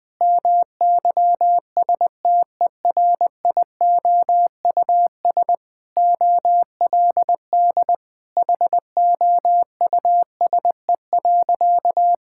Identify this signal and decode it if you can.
Morse code, heard as a keyed tone, about 20 wpm, 700 Hz. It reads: MYSTERIOUS OLD HOUSE.
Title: Sentences with 2-Word Phrases - 20wpm